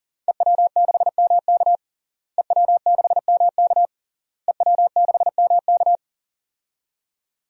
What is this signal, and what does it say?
Morse code, heard as a keyed tone, about 37 wpm, 700 Hz. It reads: EW6MK EW6MK EW6MK
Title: DXCC-100 Call Signs - 5 Characters - Repeated 3x - 40wpm